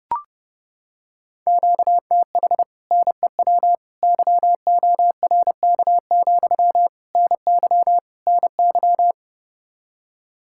Morse code, heard as a keyed tone, about 30 wpm, 700 Hz. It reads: QTH NEW YORK, NY NY